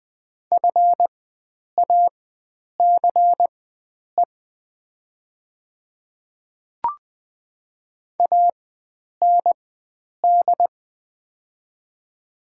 Morse code, heard as a keyed tone, about 20 wpm, spaced as for 9 wpm, 700 Hz. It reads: FACE AND
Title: 4x Character Spacing - Top 300 Words - 20wpm